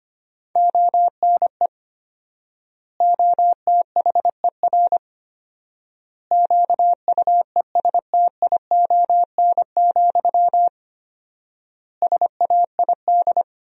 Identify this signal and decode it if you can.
Morse code, heard as a keyed tone, about 25 wpm, 700 Hz. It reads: ONE OTHER QUESTION, SAID
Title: 4x Word-Spacing - Sentences from Top 400 Words - 25wpm